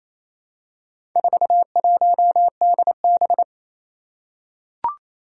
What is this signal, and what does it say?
Morse code, heard as a keyed tone, about 27 wpm, 700 Hz. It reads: 41DB
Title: Four Challenging Letter-Number Combinations 28wpm